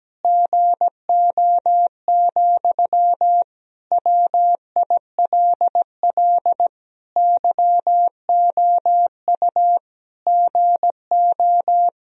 Morse code, heard as a keyed tone, about 17 wpm, 700 Hz. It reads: GO, WILL YOU GO
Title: Sentences from Top 1000 Words - Review Part 2 - 17wpm